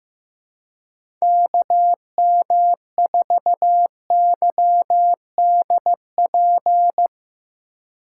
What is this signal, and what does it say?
Morse code, heard as a keyed tone, about 15 wpm, 700 Hz. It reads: KM4YDP
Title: USA Call Signs - 6 Characters - 4x Character Spacing - 15wpm